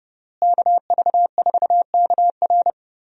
Morse code, heard as a keyed tone, about 30 wpm, 700 Hz. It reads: KV4KR
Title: USA Call Signs - 5 Characters - Repeated 3x - 30wpm